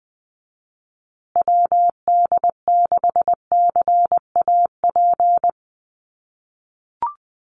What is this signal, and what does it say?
Morse code, heard as a keyed tone, about 20 wpm, 700 Hz. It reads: WD6CAP
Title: USA Call Signs Repeated 2x 20wpm